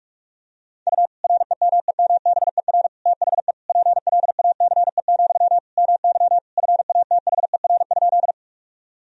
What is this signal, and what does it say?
Morse code, heard as a keyed tone, about 42 wpm, 700 Hz. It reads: U REMEMBER THE PLACE, MY FATHER?
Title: Sentences from Top 400 Words and Abbreviations 45wpm